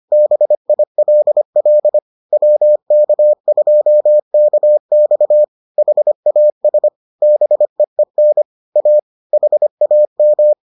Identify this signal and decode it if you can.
Morse code, heard as a keyed tone, about 25 wpm, 600 Hz. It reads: BILL WK2KX HAS BEEN A HAM